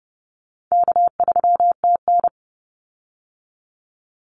Morse code, heard as a keyed tone, about 30 wpm, 700 Hz. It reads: K3TN
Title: Easy Call Signs Repeated 2x - 30wpm